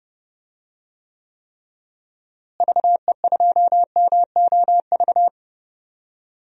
Morse code, heard as a keyed tone, about 31 wpm, 700 Hz. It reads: VE2MOV
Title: DXCC-100 Call Signs - 6 Characters - Repeated 3x - 30wpm